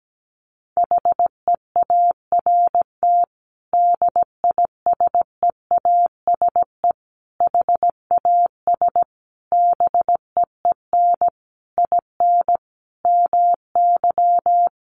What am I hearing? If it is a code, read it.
Morse code, heard as a keyed tone, about 17 wpm, 700 Hz. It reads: HEART DISEASE HAS BEEN IN MY
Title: Sentences from Top 1200 Words 17wpm